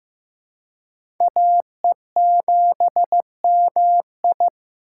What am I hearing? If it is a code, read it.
Morse code, heard as a keyed tone, about 15 wpm, 700 Hz. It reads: AE7MI
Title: USA Call Signs - 5 Characters - 4x Character Spacing - 15wpm